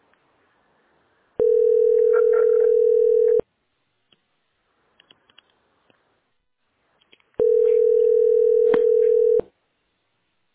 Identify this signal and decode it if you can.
Ringback tone (North America), 440 + 480 Hz.